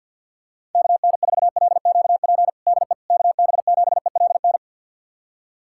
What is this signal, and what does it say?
Morse code, heard as a keyed tone, about 49 wpm, 700 Hz. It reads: KN4LQP DE KB7ELN
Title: USA Callsign DE Callsign 50wpm